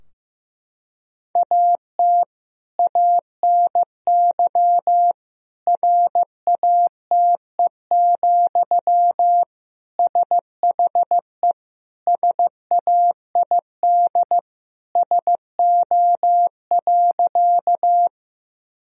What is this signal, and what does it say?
Morse code, heard as a keyed tone, about 15 wpm, 700 Hz. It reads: AT ANY RATE, SHE SAID SO.
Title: Sentences from Top 1000 Words - Review Part 4 - 15wpm